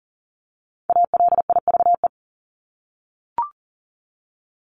Morse code, heard as a keyed tone, about 40 wpm, 700 Hz. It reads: ALIVE
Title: Top 1751-2000 Words 40wpm